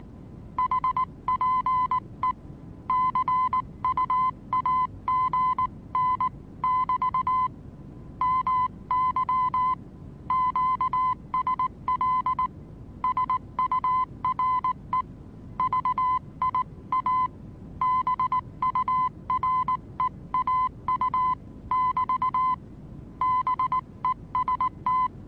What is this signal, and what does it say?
Morse code, heard as a keyed tone, about 19 wpm, 1020 Hz. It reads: HPE CUAGN<BT> MY QSL SURE VIA BUREAU<BT> BEST
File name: repos / Morsekurs / DJ4UF Morsen26